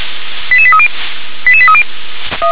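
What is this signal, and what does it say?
Five-tone selective call (ZVEI), heard as quick runs of five tones, tones of about 70 ms. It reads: 88830 88830
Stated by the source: ZVEI five-tone call 88832 received by SDR.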